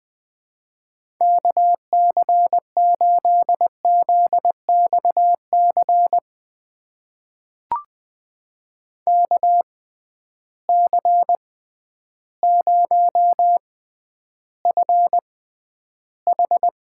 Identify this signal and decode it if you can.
Morse code, heard as a keyed tone, about 21 wpm, spaced as for 11 wpm, 700 Hz. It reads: KC8ZXC KC0FH
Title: USA Call Signs - 6 Characters - 6x Character Spacing - 20wpm